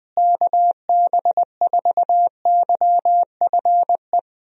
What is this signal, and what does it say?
Morse code, heard as a keyed tone, about 20 wpm, 700 Hz. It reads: KB4YFE
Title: USA Call Signs - 6 Characters - Repeated 2x - 20wpm